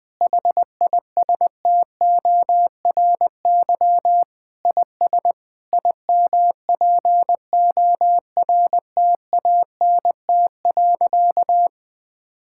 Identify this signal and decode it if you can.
Morse code, heard as a keyed tone, about 20 wpm, 700 Hz. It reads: HISTORY IS IMPORTANT.
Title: Sentences with 8-Word Phrases - 20wpm